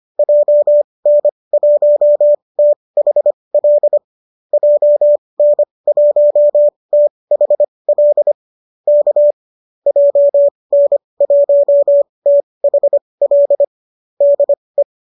Morse code, heard as a keyed tone, about 25 wpm, spaced as for 20 wpm, 590 Hz. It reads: JN1THL JN1THL K JN1THL DE
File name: qso205 20